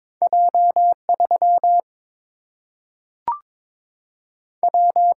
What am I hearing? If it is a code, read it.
Morse code, heard as a keyed tone, about 22 wpm, 700 Hz. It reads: J 3 W
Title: Single-Letter Single-Number 22wpm